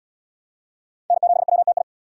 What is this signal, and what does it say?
Morse code, heard as a keyed tone, about 74 wpm, 700 Hz. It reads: N7WTI